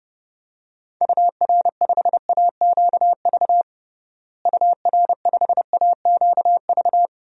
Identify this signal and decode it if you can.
Morse code, heard as a keyed tone, about 32 wpm, 700 Hz. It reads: UR5AQV UR5AQV
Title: DXCC-100 Call Signs - 6 Characters - Repeated 2x - 30wpm